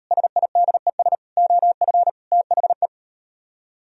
Morse code, heard as a keyed tone, about 38 wpm, 700 Hz. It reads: SIDES OF THE